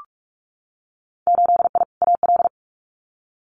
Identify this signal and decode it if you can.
Morse code, heard as a keyed tone, about 44 wpm, 700 Hz. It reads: 8I AL
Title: Winter Field Day Exchanges Encore 45wpm